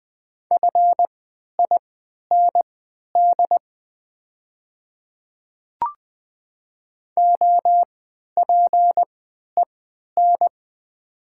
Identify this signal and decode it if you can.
Morse code, heard as a keyed tone, about 20 wpm, spaced as for 11 wpm, 700 Hz. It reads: FIND OPEN
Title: 3x Character Spacing - Top 300 Words - 20wpm